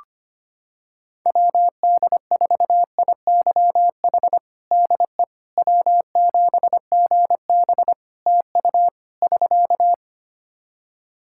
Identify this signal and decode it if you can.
Morse code, heard as a keyed tone, about 25 wpm, 700 Hz. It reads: WD4IYH DE W7GB TU <SK>
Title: Practice QSO - Exchange 4 - Ending - 25wpm